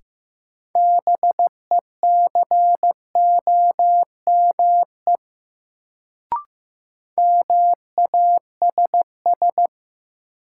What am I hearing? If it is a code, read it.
Morse code, heard as a keyed tone, about 15 wpm, 700 Hz. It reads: BECOME MASS
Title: Sets of 5 Words Encore - New Word Drill - 15wpm